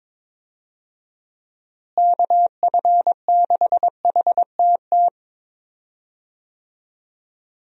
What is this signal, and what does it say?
Morse code, heard as a keyed tone, about 22 wpm, 700 Hz. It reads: KF6HTT
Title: USA Call Signs - 6 Characters - Repeated 3x - 22wpm